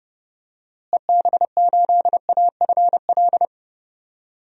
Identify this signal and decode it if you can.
Morse code, heard as a keyed tone, about 32 wpm, 700 Hz. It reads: EB8AFL
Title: DXCC-100 Call Signs - 6 Characters - Repeated 2x - 30wpm